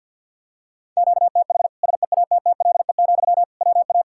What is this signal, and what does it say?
Morse code, heard as a keyed tone, about 52 wpm, 700 Hz. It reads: QTH SEATTLE, WA
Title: Practice QSO - Exchange 1 - Beginning - 50wpm